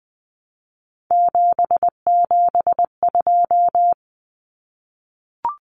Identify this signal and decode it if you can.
Morse code, heard as a keyed tone, about 20 wpm, 700 Hz. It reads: 772
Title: Numbers - 3-Digit 20wpm